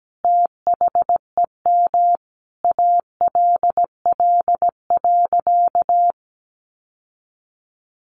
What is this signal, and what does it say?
Morse code, heard as a keyed tone, about 17 wpm, 700 Hz. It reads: THEM ALL.
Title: Sentences from Top 1400 Words 17wpm